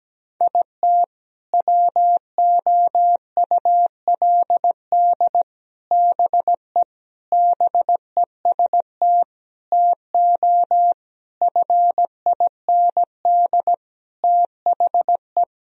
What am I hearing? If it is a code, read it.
Morse code, heard as a keyed tone, about 17 wpm, 700 Hz. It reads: IT WOULD BE BEST TO FIND THE
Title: Sentences from Top 1000 Words - Review Part 2 - 17wpm